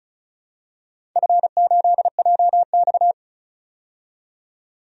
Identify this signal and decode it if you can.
Morse code, heard as a keyed tone, about 36 wpm, 700 Hz. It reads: F8JX
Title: DXCC-100 Call Signs - 4 Characters - 35wpm